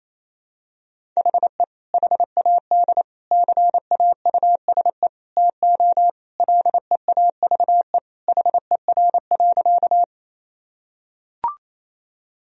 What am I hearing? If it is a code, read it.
Morse code, heard as a keyed tone, about 28 wpm, 700 Hz. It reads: HE HAD CAUSE TO LEAVE HER.
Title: Sentences from Top 1000 Words - Review Part 1 - 28wpm